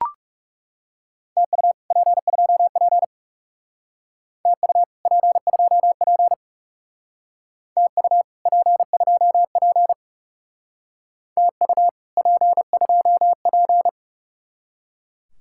Morse code, heard as a keyed tone, about 39 wpm, 700 Hz. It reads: TU P2P TU P2P TU P2P TU P2P